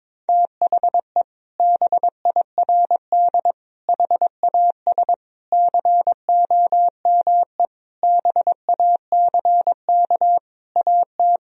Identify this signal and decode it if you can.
Morse code, heard as a keyed tone, about 22 wpm, 700 Hz. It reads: THE BIRD HAS COME BACK AT
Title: Sentences from Top 1000 Words - Review Part 2 - 22wpm